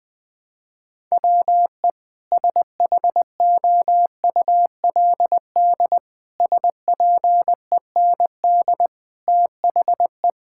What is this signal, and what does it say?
Morse code, heard as a keyed tone, about 20 wpm, 700 Hz. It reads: WE SHOULD SPEND THE
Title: Sentences from Top 1000 Words - Review Part 5 - 20wpm